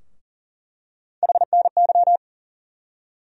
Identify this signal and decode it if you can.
Morse code, heard as a keyed tone, about 40 wpm, 700 Hz. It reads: HNY
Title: USA Call Sign Postfixes - 3 Letters 40wpm